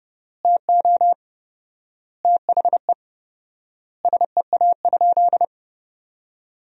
Morse code, heard as a keyed tone, about 30 wpm, 700 Hz. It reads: TO THE SEA?
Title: 4x Word-Spacing - Sentences from Top 300 Words - 30wpm